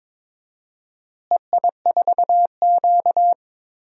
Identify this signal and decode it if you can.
Morse code, heard as a keyed tone, about 22 wpm, 700 Hz. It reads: EI4Q
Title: DXCC-100 Call Signs - 4 Characters - Repeated 3x - 22wpm